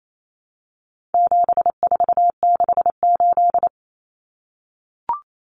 Morse code, heard as a keyed tone, about 28 wpm, 700 Hz. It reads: 7468
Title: Numbers - 4-Digit 28wpm